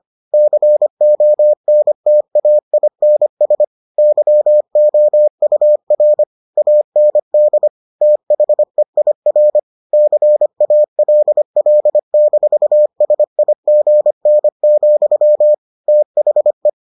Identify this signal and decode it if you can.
Morse code, heard as a keyed tone, about 25 wpm, 600 Hz. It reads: CONTAINS YOUR AND THEIR CALL-SIGN, THE